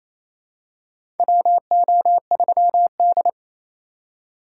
Morse code, heard as a keyed tone, about 28 wpm, 700 Hz. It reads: WO3D
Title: USA Call Signs - 4 Characters - 2x Character Spacing - 28wpm